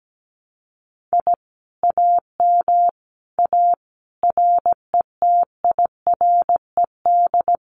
Morse code, heard as a keyed tone, about 17 wpm, 700 Hz. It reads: I AM A RETIRED
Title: Common QSO Phrases Encore 17wpm